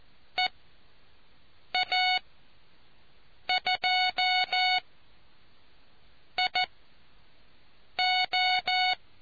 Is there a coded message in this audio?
Yes: Morse code from 0.4 to 8.9 s, heard as a keyed tone, about 14 wpm, spaced as for 5 wpm, 740 Hz. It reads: EA2IO